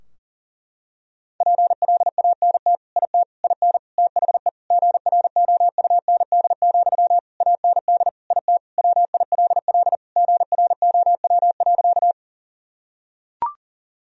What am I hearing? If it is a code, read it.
Morse code, heard as a keyed tone, about 37 wpm, 700 Hz. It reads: PLANT IT IN THE GROUND, AND IT WILL GROW.
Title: Sentences from Top 1000 Words - Review Part 2 - 40wpm